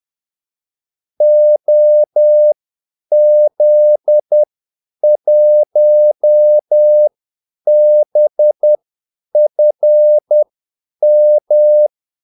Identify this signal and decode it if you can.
Morse code, heard as a keyed tone, about 10 wpm, spaced as for 8 wpm, 600 Hz. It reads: OZ1BFM
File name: Callsigns-10-8wpm